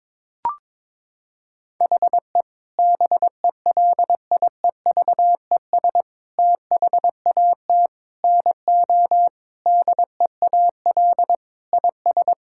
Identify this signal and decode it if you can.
Morse code, heard as a keyed tone, about 22 wpm, 700 Hz. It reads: HE BELIEVES THAT NO DEAL IS
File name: Sentences with 9-Word Phrases - 22wpm